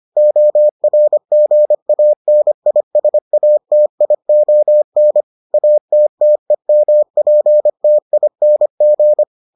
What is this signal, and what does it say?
Morse code, heard as a keyed tone, about 25 wpm, 600 Hz. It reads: ORGANISATION ATTEMPTING